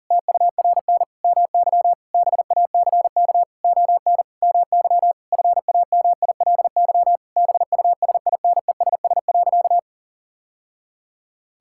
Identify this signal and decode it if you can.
Morse code, heard as a keyed tone, about 40 wpm, 700 Hz. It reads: TURN MY BACK ON MY FAMILY BUSINESS.
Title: Sentences from Top 1000 Words - Review Part 6 - 40wpm